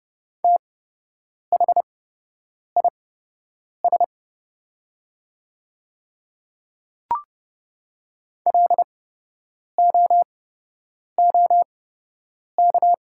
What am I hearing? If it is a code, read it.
Morse code, heard as a keyed tone, about 30 wpm, spaced as for 8 wpm, 700 Hz. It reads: THIS LOOK
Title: 8x Character Spacing - Top 100 Words - 30wpm